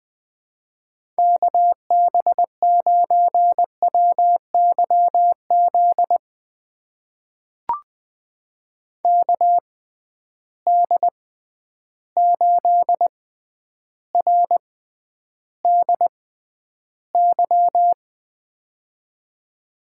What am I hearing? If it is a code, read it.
Morse code, heard as a keyed tone, about 20 wpm, spaced as for 10 wpm, 700 Hz. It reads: KB9WYZ KD8RDY
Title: USA Call Signs - 6 Characters - 6x Character Spacing - 20wpm